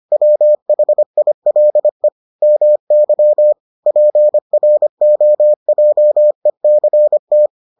Morse code, heard as a keyed tone, about 25 wpm, 600 Hz. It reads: WHILE MY PROJECT